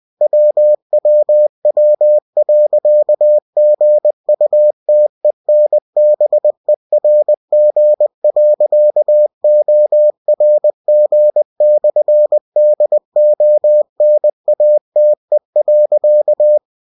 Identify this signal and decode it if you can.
Morse code, heard as a keyed tone, about 20 wpm, 600 Hz. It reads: WWW.GUTENBERG.ORG/DONATE.